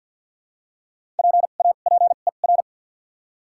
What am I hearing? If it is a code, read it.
Morse code, heard as a keyed tone, about 53 wpm, 700 Hz. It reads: P A P E R
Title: 2x Character Spacing - Top 300 Words - 50wpm